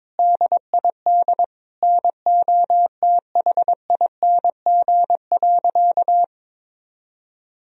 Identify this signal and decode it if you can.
Morse code, heard as a keyed tone, about 22 wpm, 700 Hz. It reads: DID NOTHING.
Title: Sentences from Top 1000 Words - Review Part 3 - 22wpm